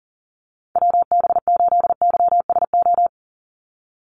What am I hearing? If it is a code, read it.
Morse code, heard as a keyed tone, about 40 wpm, 690 Hz. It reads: WB8YSO
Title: USA Call Signs 40wpm